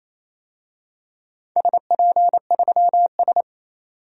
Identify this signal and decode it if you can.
Morse code, heard as a keyed tone, about 28 wpm, 700 Hz. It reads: SP3S
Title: DXCC-100 Call Signs - 4 Characters - 28wpm